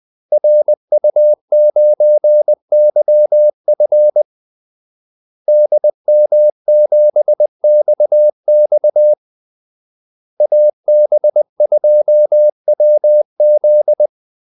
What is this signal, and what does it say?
Morse code, heard as a keyed tone, about 20 wpm, 600 Hz. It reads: RU9YF DM7XX AB2WZ
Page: Kaldesignaler 20 wpm